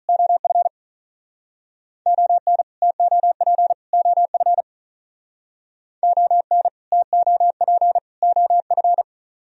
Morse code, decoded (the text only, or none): OF ON TOP OF ON TOP OF